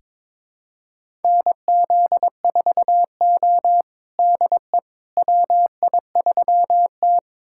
Morse code, heard as a keyed tone, about 22 wpm, 700 Hz. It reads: NZ4O DE WI3T
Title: USA Callsign DE Callsign - 4 Characters - 22wpm